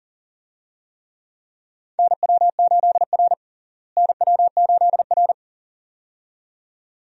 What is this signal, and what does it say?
Morse code, heard as a keyed tone, about 40 wpm, 700 Hz. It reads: NW8R NW8R
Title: USA Call Signs - 4 Characters - Repeated 2x - 40wpm